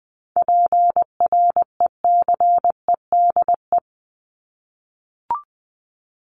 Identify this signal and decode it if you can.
Morse code, heard as a keyed tone, about 20 wpm, 700 Hz. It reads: PRECEDE
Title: ING Suffix - Top 100 Words with Base Words 20wpm